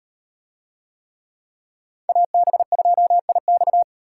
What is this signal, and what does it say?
Morse code, heard as a keyed tone, about 35 wpm, 700 Hz. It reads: AB2IX